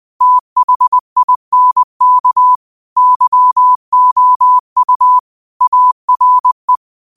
Morse code, heard as a keyed tone, about 20 wpm, 1000 Hz. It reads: THINK YOU ARE